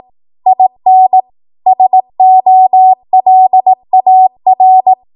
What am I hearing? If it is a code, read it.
Morse code, heard as a keyed tone, about 18 wpm, 750 Hz. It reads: IN SOLAR
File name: Morse / 18WPM